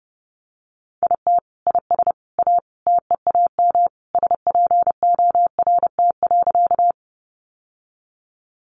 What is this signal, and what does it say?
Morse code, heard as a keyed tone, about 30 wpm, 700 Hz. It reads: IT IS A TEAM SPORT.
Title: Sentences from Top 1500 Words 30wpm